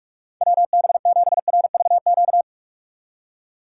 Morse code, heard as a keyed tone, about 44 wpm, 700 Hz. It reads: WB7RVQ
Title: USA Call Signs - 6 Characters - 6x Character Spacing - 45wpm